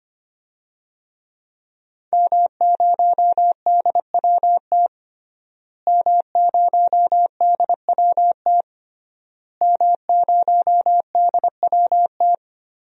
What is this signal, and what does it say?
Morse code, heard as a keyed tone, about 25 wpm, 700 Hz. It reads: M0DWT M0DWT M0DWT
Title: DXCC-100 Call Signs - 5 Characters - Repeated 3x - 25wpm